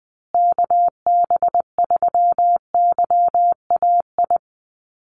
Morse code, heard as a keyed tone, about 20 wpm, 700 Hz. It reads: KB3YAI
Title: USA Call Signs Repeated 2x 20wpm